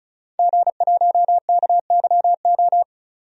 Morse code, heard as a keyed tone, about 36 wpm, 700 Hz. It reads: G1KYO
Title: DXCC-100 Call Signs - 5 Characters - 35wpm